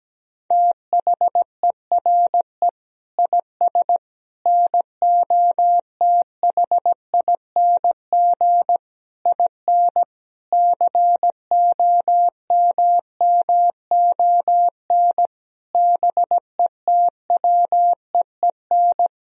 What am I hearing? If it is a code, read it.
Morse code, heard as a keyed tone, about 17 wpm, 700 Hz. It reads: THERE IS NOTHING IN COMMON BETWEEN
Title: Sentences from Top 1000 Words - Review Part 3 - 17wpm